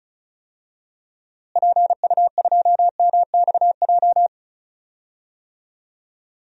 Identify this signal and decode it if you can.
Morse code, heard as a keyed tone, about 36 wpm, 700 Hz. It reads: PU2MXJ
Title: DXCC-100 Call Signs - 6 Characters - Repeated 2x - 35wpm